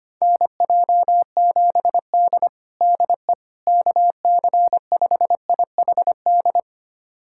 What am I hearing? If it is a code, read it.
Morse code, heard as a keyed tone, about 25 wpm, 700 Hz. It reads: NJ7D DE KC5IHD
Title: USA Callsign DE Callsign 25wpm